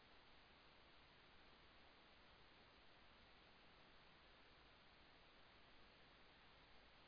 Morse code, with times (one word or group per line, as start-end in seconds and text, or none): none